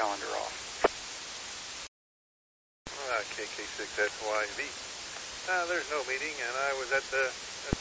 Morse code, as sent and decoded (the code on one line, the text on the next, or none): none